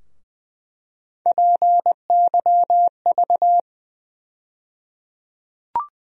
Morse code, heard as a keyed tone, about 20 wpm, 700 Hz. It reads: PYV
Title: USA Call Sign Postfixes - 3 Letters 20wpm